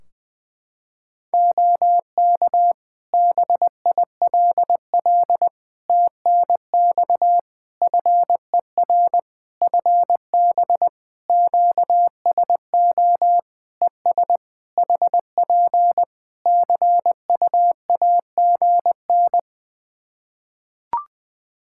Morse code, heard as a keyed tone, about 20 wpm, 700 Hz. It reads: OK BILL TNX FER FB QSO ES HP CUAGN
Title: Practice QSO - Exchange 4 - Ending - 20wpm